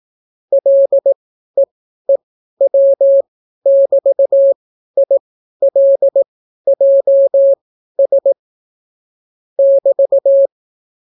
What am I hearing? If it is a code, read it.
Morse code, heard as a keyed tone, about 18 wpm, spaced as for 12 wpm, 550 Hz. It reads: LEEW<BT>ILJS <BT>